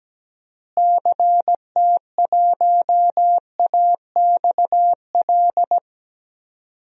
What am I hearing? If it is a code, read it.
Morse code, heard as a keyed tone, about 17 wpm, 700 Hz. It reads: CT1AXL
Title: DXCC-100 Call Signs - 6 Characters - Repeated 3x - 17wpm